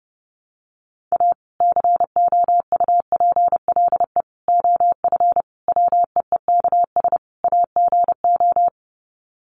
Morse code, heard as a keyed tone, about 30 wpm, 700 Hz. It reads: A COUPLE OF WEEKS AGO
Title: Sets of 5 Words 30wpm